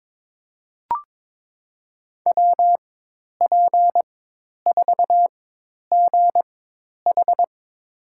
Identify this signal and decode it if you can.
Morse code, heard as a keyed tone, about 22 wpm, spaced as for 10 wpm, 700 Hz. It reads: WP4GH
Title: USA Call Signs - 5 Characters - 4x Character Spacing - 22wpm